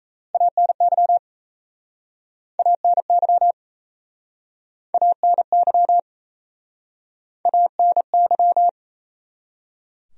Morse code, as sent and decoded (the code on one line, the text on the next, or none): .- -. -.-- / .- -. -.-- / .- -. -.-- / .- -. -.--
ANY ANY ANY ANY